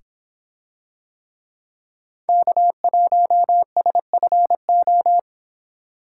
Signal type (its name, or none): Morse code